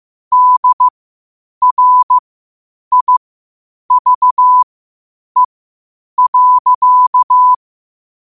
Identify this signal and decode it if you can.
Morse code, heard as a keyed tone, about 15 wpm, spaced as for 8 wpm, 1000 Hz. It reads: DRIVE.
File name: qotdfiles / 8wpm